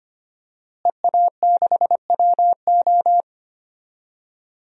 Morse code, heard as a keyed tone, about 25 wpm, 700 Hz. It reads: EA6WO
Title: DXCC-100 Call Signs - 5 Characters - Repeated 3x - 25wpm